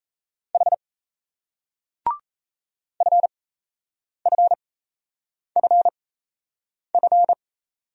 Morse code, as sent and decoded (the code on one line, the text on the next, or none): .... / ..-. ..-. ..-. ..-.
H FFFF